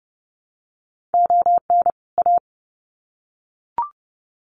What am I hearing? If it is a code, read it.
Morse code, heard as a keyed tone, about 30 wpm, 700 Hz. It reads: ON A
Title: Sets of 2 Words 30wpm